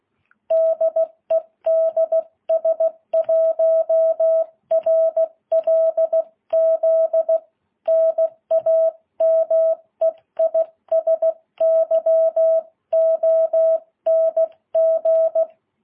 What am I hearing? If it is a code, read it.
Morse code, heard as a keyed tone, about 16 wpm, 660 Hz. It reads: DE DS1RLZ NAME IS YONG
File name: jh8cbh